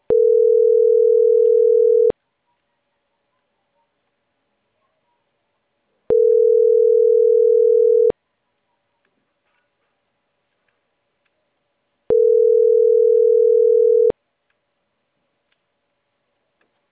Ringback tone (North America), 440 + 480 Hz.